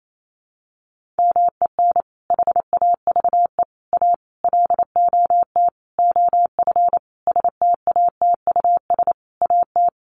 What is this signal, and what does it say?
Morse code, heard as a keyed tone, about 28 wpm, 700 Hz. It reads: MEN HAVE A LOT OF STATUS AT
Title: Sentences from Top 1400 Words 28wpm